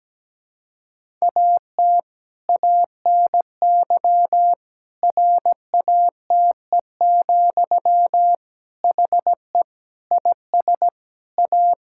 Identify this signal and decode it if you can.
Morse code, heard as a keyed tone, about 17 wpm, 700 Hz. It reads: AT ANY RATE, HE IS A
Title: Sentences from Top 1000 Words - Review Part 5 - 17wpm